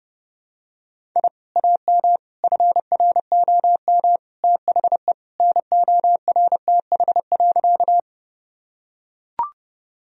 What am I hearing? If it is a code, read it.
Morse code, heard as a keyed tone, about 30 wpm, 700 Hz. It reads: I AM FROM THE NORTH.
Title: Sentences from Top 1000 Words - Review Part 1 - 30wpm